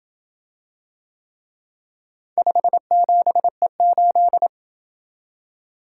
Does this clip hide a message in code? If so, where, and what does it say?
Yes: Morse code from 2.4 to 4.5 s, heard as a keyed tone, about 27 wpm, 700 Hz. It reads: 57E8